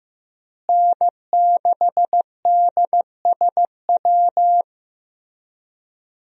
Morse code, heard as a keyed tone, about 15 wpm, 700 Hz. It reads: N6DSW
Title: USA Call Signs - 5 Characters - Repeated 3x - 15wpm